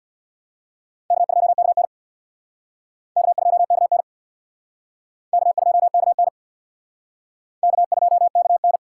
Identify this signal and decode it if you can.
Morse code, heard as a keyed tone, about 64 wpm, 700 Hz. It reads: K2KN K2KN K2KN K2KN